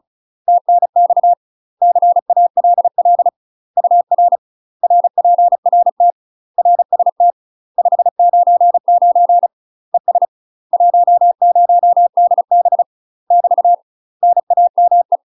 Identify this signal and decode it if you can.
Morse code, heard as a keyed tone, about 34 wpm, 700 Hz. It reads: TNX CALL UR RPRT RST 599 ES 10DB <BT> NAME
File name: UK MASTER QSOs TRAINING / 35WPM / UK MASTER QSOs 0001